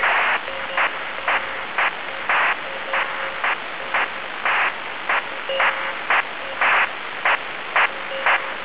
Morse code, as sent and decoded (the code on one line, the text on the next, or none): none